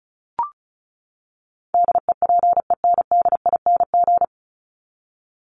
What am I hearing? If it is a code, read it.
Morse code, heard as a keyed tone, about 34 wpm, 700 Hz. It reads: DEPENDING
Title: ING Suffix - Top 100 Words with Base Words 35wpm